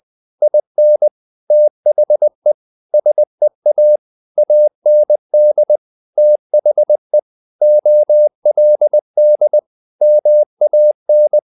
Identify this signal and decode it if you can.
Morse code, heard as a keyed tone, about 20 wpm, 600 Hz. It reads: IN THE SEA AND THE OLD MAN
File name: the old man and the sea / the old man and the sea part 1 20wpm